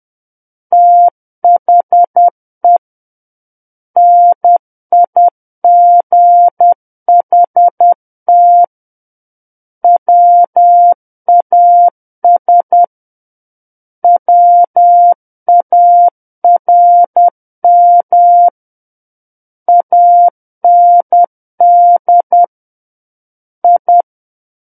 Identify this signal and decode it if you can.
Morse code, heard as a keyed tone, about 10 wpm, 700 Hz. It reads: THE NIGHT WAS WARM AND I
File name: Book One / 01-Chapter 1 - 10 WPM